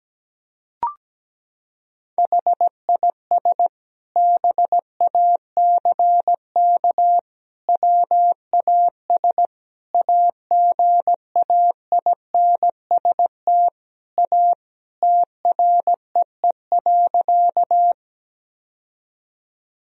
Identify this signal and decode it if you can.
Morse code, heard as a keyed tone, about 17 wpm, 700 Hz. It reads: HIS BACK WAS AGAINST A TREE.
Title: Sentences from Top 1000 Words - Review Part 1 - 17wpm